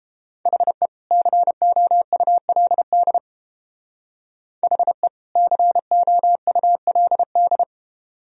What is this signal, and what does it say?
Morse code, heard as a keyed tone, about 29 wpm, 700 Hz. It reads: HE COULD HE COULD